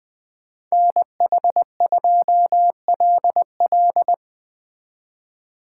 Morse code, heard as a keyed tone, about 20 wpm, 700 Hz. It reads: NH2LL
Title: Speed Racing - USA 2x2 Call Signs 17wpm